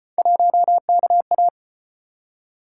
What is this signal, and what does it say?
Morse code, heard as a keyed tone, about 32 wpm, 700 Hz. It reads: 1KA